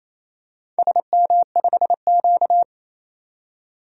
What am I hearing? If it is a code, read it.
Morse code, heard as a keyed tone, about 28 wpm, 700 Hz. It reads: SM5Q
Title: DXCC-100 Call Signs - 4 Characters - 28wpm